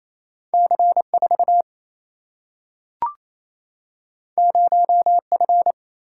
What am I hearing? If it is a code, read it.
Morse code, heard as a keyed tone, about 28 wpm, 700 Hz. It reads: C 4 0 F
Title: Single-Letter Single-Number 28wpm